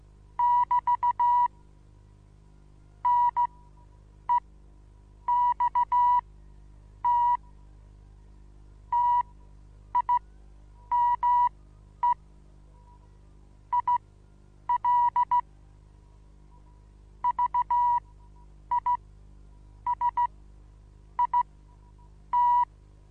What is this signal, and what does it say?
Morse code, heard as a keyed tone, about 16 wpm, spaced as for 8 wpm, 970 Hz. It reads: <BT> NEXT TIME IL VISIT
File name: Onlinelehrgang / morsen / DJ4UF Morsen25